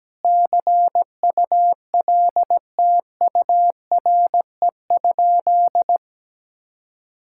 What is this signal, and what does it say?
Morse code, heard as a keyed tone, about 17 wpm, 700 Hz. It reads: CULTURE?
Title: Sentences from Top 1000 Words - Review Part 6 - 17wpm